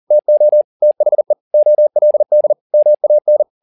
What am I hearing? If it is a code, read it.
Morse code, heard as a keyed tone, about 40 wpm, 600 Hz. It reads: TO THE OLD MAN